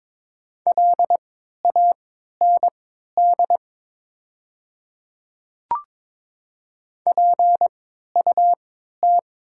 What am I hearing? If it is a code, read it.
Morse code, heard as a keyed tone, about 22 wpm, spaced as for 12 wpm, 700 Hz. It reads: LAND PUT
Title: 3x Character Spacing - Top 500 Words - 22wpm